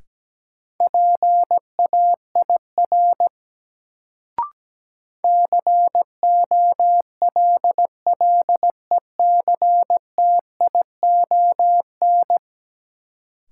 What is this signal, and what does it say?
Morse code, heard as a keyed tone, about 17 wpm, 700 Hz. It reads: PAIR COLLECTION
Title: Sentences from Top 1200 Words - New Word Drill Alt - 17wpm